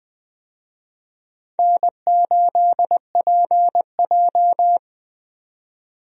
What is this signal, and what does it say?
Morse code, heard as a keyed tone, about 20 wpm, 700 Hz. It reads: N8PJ
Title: USA Call Signs - 4 Characters - 20wpm